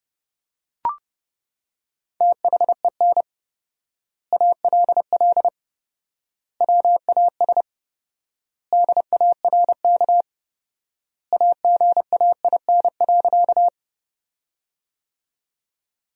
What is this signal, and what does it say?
Morse code, heard as a keyed tone, about 31 wpm, 700 Hz. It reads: THEN ALL WAS DARK AGAIN.
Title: 4x Word-Spacing - Sentences from Top 500 Words - 30wpm